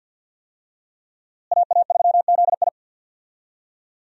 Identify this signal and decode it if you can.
Morse code, heard as a keyed tone, about 51 wpm, 700 Hz. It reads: AA3ZI
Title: USA Call Signs - 5 Characters - Repeated 3x - 50wpm